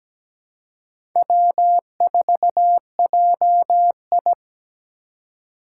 Morse code, heard as a keyed tone, about 17 wpm, 700 Hz. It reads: W4JI
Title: USA Call Signs - 4 Characters - 3x Character Spacing - 17wpm